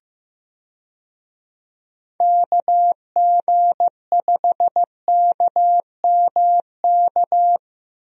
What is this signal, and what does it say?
Morse code, heard as a keyed tone, about 15 wpm, 700 Hz. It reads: KG5KMK
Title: USA Call Signs - 6 Characters - Repeated 2x - 15wpm